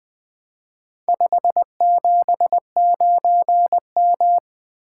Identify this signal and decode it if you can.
Morse code, heard as a keyed tone, about 20 wpm, 700 Hz. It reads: 579M
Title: Four Letter-Number Combinations - 20wpm